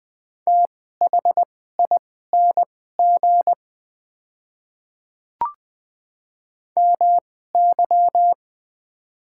Morse code, heard as a keyed tone, about 21 wpm, spaced as for 15 wpm, 700 Hz. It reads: THING MY
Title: 2x Character Spacing - Top 100 Words - 20wpm